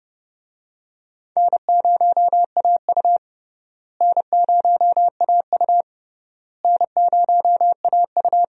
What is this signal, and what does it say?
Morse code, heard as a keyed tone, about 30 wpm, 700 Hz. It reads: N0AU N0AU N0AU
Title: USA Call Signs - 4 Characters - Repeated 3x - 30wpm